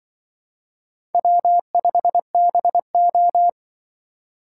Morse code, heard as a keyed tone, about 23 wpm, 690 Hz. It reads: W5BO
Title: Speed Racing - USA 1x2 Call Signs 20wpm